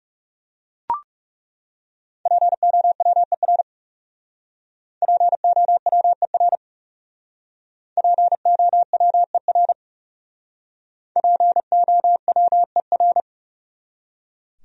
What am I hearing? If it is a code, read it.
Morse code, heard as a keyed tone, about 40 wpm, 700 Hz. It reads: POWER POWER POWER POWER